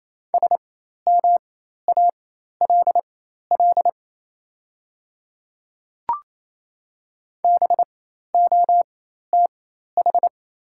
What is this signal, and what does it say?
Morse code, heard as a keyed tone, about 28 wpm, spaced as for 13 wpm, 700 Hz. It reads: SMALL BOTH
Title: 4x Character Spacing - Top 300 Words - 28wpm